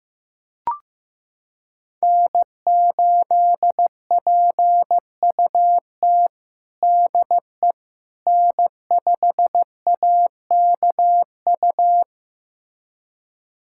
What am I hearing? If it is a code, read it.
Morse code, heard as a keyed tone, about 15 wpm, 700 Hz. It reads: N8PUT DE N5AKU
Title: USA Callsign DE Callsign - 5 Characters - 15wpm